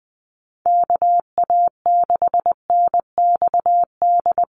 Morse code, heard as a keyed tone, about 20 wpm, 700 Hz. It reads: KA6NXD
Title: USA Call Signs 20wpm